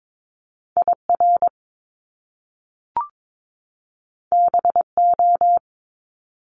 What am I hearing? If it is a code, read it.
Morse code, heard as a keyed tone, about 22 wpm, 700 Hz. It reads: IR BO
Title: Top Two Letter Combinations in Words 22wpm